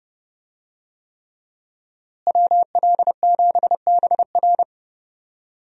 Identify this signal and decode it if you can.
Morse code, heard as a keyed tone, about 30 wpm, 700 Hz. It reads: WL7BR
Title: USA Call Signs - 5 Characters - Repeated 3x - 30wpm